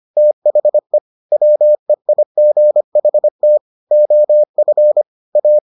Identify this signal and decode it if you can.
Morse code, heard as a keyed tone, about 25 wpm, 600 Hz. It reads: THE WEIGHT OF A